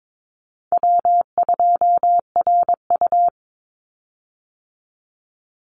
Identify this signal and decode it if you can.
Morse code, heard as a keyed tone, about 22 wpm, 700 Hz. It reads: W2RU
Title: Easy Call Signs - 22wpm